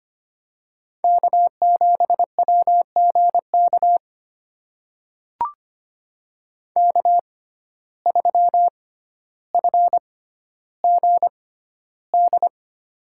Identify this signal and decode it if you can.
Morse code, heard as a keyed tone, about 25 wpm, spaced as for 13 wpm, 700 Hz. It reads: K7WGK K3FGD
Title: USA Call Signs - 5 Characters - 6x Character Spacing - 25wpm